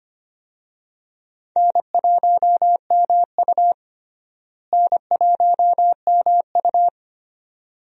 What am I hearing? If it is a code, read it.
Morse code, heard as a keyed tone, about 25 wpm, 700 Hz. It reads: N1MU N1MU
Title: USA Call Signs - 4 Characters - Repeated 2x - 25wpm